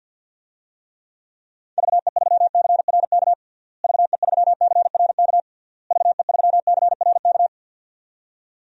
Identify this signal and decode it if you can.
Morse code, heard as a keyed tone, about 52 wpm, 700 Hz. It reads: VE3CRK VE3CRK VE3CRK
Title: DXCC-100 Call Signs - 6 Characters - Repeated 3x - 50wpm